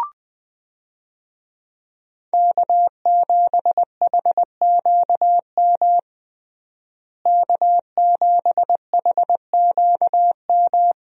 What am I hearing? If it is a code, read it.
Morse code, heard as a keyed tone, about 21 wpm, 700 Hz. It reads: K7HQM K7HQM
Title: USA Call Signs - 5 Characters - Repeated 2x - 20wpm